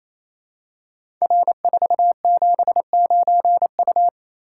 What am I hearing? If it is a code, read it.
Morse code, heard as a keyed tone, about 29 wpm, 700 Hz. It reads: R479U
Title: Code Groups - 5 Letter:Numbers - 28wpm